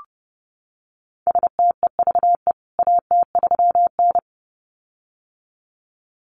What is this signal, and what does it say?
Morse code, heard as a keyed tone, about 30 wpm, 700 Hz. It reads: STEVE AT3N
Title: CWT Exchange 30wpm